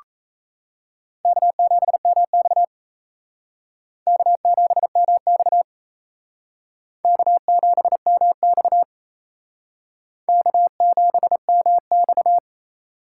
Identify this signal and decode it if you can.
Morse code, heard as a keyed tone, about 36 wpm, 700 Hz. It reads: K7MX K7MX K7MX K7MX